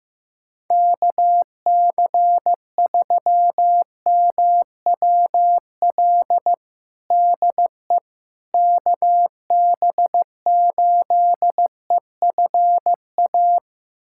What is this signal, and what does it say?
Morse code, heard as a keyed tone, about 15 wpm, 700 Hz. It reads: KC3MWL DE KB8EFA
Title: USA Callsign DE Callsign - 6 Characters - 15wpm